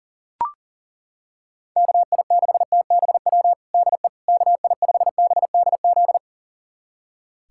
Morse code, heard as a keyed tone, about 40 wpm, 700 Hz. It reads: KI6TBW DE KI5BDZ
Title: USA Callsign DE Callsign - 6 Characters - 40wpm